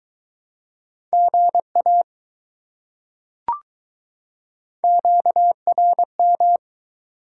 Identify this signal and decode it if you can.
Morse code, heard as a keyed tone, about 21 wpm, 700 Hz. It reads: GA QRM